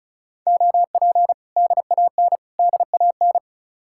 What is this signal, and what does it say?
Morse code, heard as a keyed tone, about 44 wpm, 700 Hz. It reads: OP DAN DAN